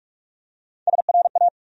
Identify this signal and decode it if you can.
Morse code, heard as a keyed tone, about 44 wpm, 700 Hz. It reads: SRA
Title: USA Call Sign Postfixes - 3 Letters 45wpm